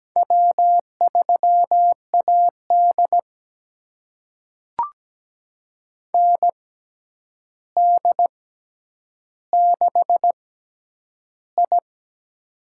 Morse code, heard as a keyed tone, about 17 wpm, spaced as for 9 wpm, 700 Hz. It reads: W3AD ND6I
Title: USA Call Signs - 4 Characters - 6x Character Spacing - 17wpm